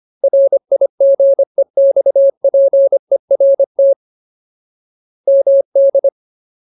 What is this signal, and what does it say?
Morse code, heard as a keyed tone, about 25 wpm, 550 Hz. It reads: RIGEXPERT MD
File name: Ultimate Word List / 25WPM Ultimate